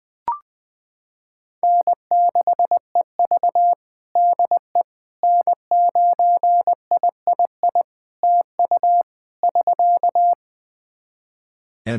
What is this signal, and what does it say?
Morse code, heard as a keyed tone, about 20 wpm, 700 Hz. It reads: N6EV DE N9III TU <SK>
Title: Practice QSO - Exchange 4 - Ending - 20wpm